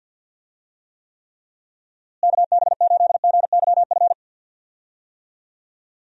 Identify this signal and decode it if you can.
Morse code, heard as a keyed tone, about 52 wpm, 700 Hz. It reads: KB8GYF
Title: USA Call Signs - 6 Characters - Repeated 3x - 50wpm